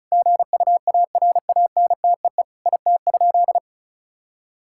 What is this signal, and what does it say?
Morse code, heard as a keyed tone, about 34 wpm, 700 Hz. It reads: GUARANTEE IT?
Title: Sentences from Top 1600 Words 35wpm